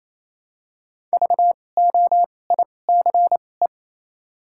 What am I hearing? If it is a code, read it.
Morse code, heard as a keyed tone, about 28 wpm, spaced as for 20 wpm, 700 Hz. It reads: VOICE
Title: 2x Character Spacing - Top 500 Words - 28wpm